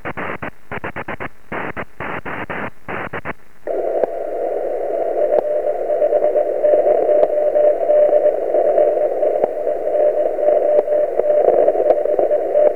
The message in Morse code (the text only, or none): none